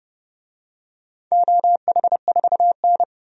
Morse code, heard as a keyed tone, about 30 wpm, 700 Hz. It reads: OH4N
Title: DXCC-100 Call Signs - 4 Characters - 30wpm